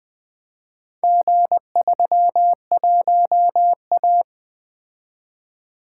Morse code, heard as a keyed tone, about 20 wpm, 700 Hz. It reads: G31A
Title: Four Letter-Number Combinations - 20wpm